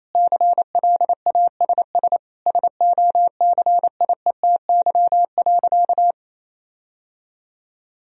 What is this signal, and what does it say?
Morse code, heard as a keyed tone, about 28 wpm, 700 Hz. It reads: CLASS SOCIETY.
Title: Sentences from Top 1000 Words - Review Part 4 - 28wpm